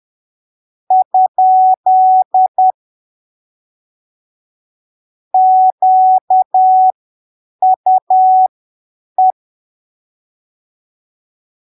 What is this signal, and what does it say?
Morse code, heard as a keyed tone, about 10 wpm, spaced as for 7 wpm, 750 Hz. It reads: ? QUE
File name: arquivos / texto / texto 16 10ppm